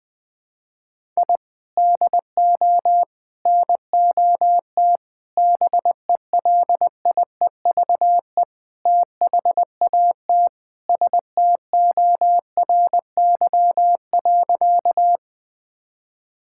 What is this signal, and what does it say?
Morse code, heard as a keyed tone, about 20 wpm, 700 Hz. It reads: I DO NOT BELIEVE THAT STORY.
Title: Sentences with 5-Word Phrases - 20wpm